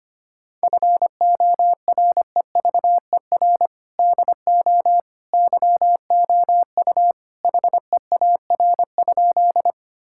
Morse code, heard as a keyed tone, about 25 wpm, 700 Hz. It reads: FOREVER DO YOU HEAR?
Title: Sentences from Top 1600 Words 25wpm